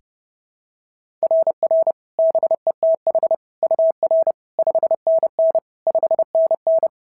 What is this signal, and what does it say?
Morse code, heard as a keyed tone, about 32 wpm, 690 Hz. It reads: RR BETH UR 5NN 5NN
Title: SOTA QSO Practice 30wpm